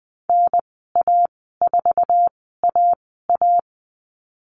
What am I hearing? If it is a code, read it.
Morse code, heard as a keyed tone, about 20 wpm, spaced as for 15 wpm, 700 Hz. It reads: NA4AA
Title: 055 Lesson 10g - Letter r - Call Signs 20wpm